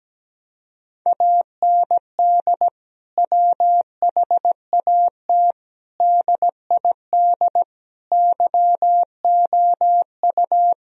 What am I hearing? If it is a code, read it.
Morse code, heard as a keyed tone, about 17 wpm, 700 Hz. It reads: AND WHAT DID YOU
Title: Sentences from Top 1000 Words - Review Part 3 - 17wpm